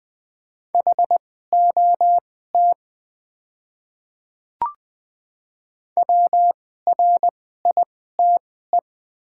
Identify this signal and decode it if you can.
Morse code, heard as a keyed tone, about 20 wpm, spaced as for 14 wpm, 700 Hz. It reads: HOT WRITE
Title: 2x Character Spacing - Top 200 Words - 20wpm